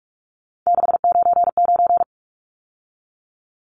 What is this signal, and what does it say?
Morse code, heard as a keyed tone, about 44 wpm, 700 Hz. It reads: 699
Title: Numbers - 3-Digit 45wpm